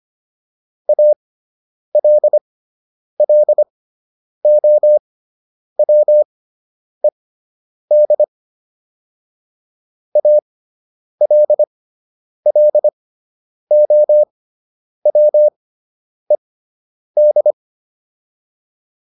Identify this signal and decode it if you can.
Morse code, heard as a keyed tone, about 25 wpm, spaced as for 9 wpm, 600 Hz. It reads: ALLOWED ALLOWED